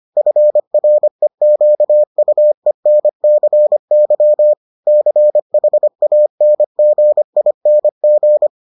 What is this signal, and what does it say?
Morse code, heard as a keyed tone, about 25 wpm, 600 Hz. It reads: FREQUENCY CHANGING